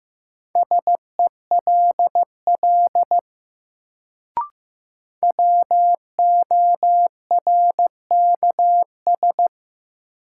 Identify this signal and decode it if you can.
Morse code, heard as a keyed tone, about 15 wpm, 700 Hz. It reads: SELL WORKS
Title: Sentences from Top 1100 Words - New Word Drill Alt - 15wpm